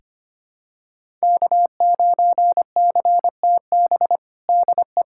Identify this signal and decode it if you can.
Morse code, heard as a keyed tone, about 25 wpm, 700 Hz. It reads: K9CTB DE
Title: USA Callsign DE Callsign - 5 Characters - 25wpm